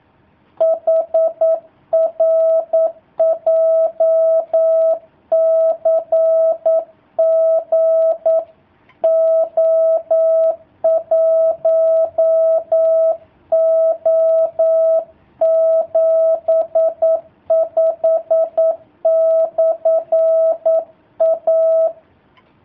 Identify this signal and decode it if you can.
Morse code, heard as a keyed tone, about 9 wpm, 650 Hz. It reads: HR JCG O1O75/A
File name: jh8cbh / 2019cw / sp09- w